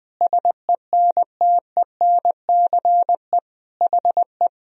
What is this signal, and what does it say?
Morse code, heard as a keyed tone, about 21 wpm, 700 Hz. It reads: SENTENCE HE
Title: Sentences from Top 1000 Words - Review Part 1 - 20wpm